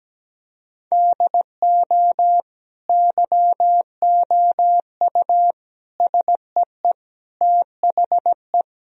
Morse code, heard as a keyed tone, about 17 wpm, 700 Hz. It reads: DO YOU SEE THE
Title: Sentences from Top 1000 Words - Review Part 2 - 17wpm